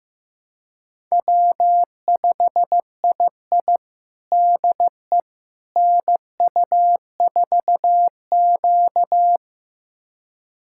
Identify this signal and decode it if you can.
Morse code, heard as a keyed tone, about 15 wpm, 700 Hz. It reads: W5II DE NU4Q
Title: USA Callsign DE Callsign - 4 Characters - 15wpm